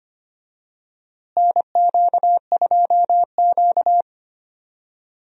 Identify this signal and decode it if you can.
Morse code, heard as a keyed tone, about 25 wpm, 700 Hz. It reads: NQ2Q
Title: USA Call Signs - 4 Characters - Repeated 3x - 25wpm